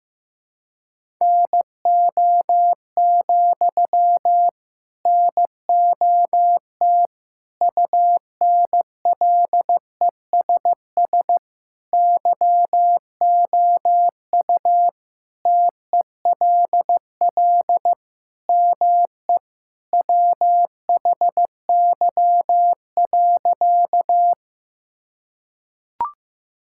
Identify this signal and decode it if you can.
Morse code, heard as a keyed tone, about 15 wpm, 700 Hz. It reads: NO, NOT UNLESS YOU TELL ME WHY.
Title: Sentences from Top 1000 Words - Review Part 6 - 15wpm